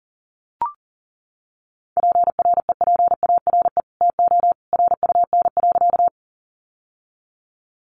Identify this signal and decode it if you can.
Morse code, heard as a keyed tone, about 40 wpm, 700 Hz. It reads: PREPARE TO RUN.
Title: Sentences from Top 900 Words 40wpm